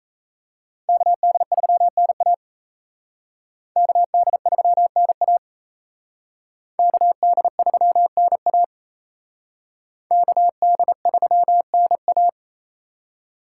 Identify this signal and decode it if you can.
Morse code, heard as a keyed tone, about 34 wpm, 700 Hz. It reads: KD3NA KD3NA KD3NA KD3NA
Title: Speed Racing - USA 2x2 Call Signs 28wpm